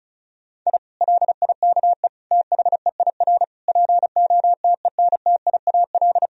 Morse code, heard as a keyed tone, about 34 wpm, 700 Hz. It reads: I LIKE THEIR POTENTIAL
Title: Sentences from Top 1000 Words - Review Part 4 - 35wpm